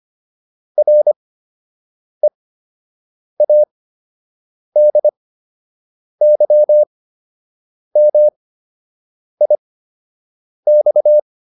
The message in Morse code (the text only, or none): READYMIX